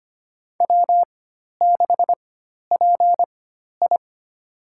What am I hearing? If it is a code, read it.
Morse code, heard as a keyed tone, about 25 wpm, spaced as for 12 wpm, 700 Hz. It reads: W6PI
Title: USA Call Signs - 4 Characters - 4x Character Spacing - 25wpm